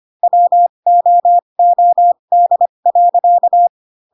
Morse code, heard as a keyed tone, about 25 wpm, spaced as for 22 wpm, 700 Hz. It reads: WOOD.